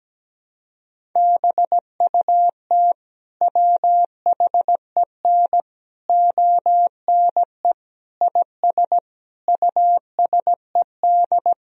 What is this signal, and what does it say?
Morse code, heard as a keyed tone, about 17 wpm, 700 Hz. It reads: BUT WHEN ONE IS USED
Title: Sentences from Top 1000 Words - Review Part 1 - 17wpm